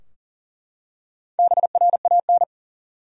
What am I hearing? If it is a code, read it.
Morse code, heard as a keyed tone, about 37 wpm, 700 Hz. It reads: BRAN
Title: USA Common Names - Max 5 Letters - 40wpm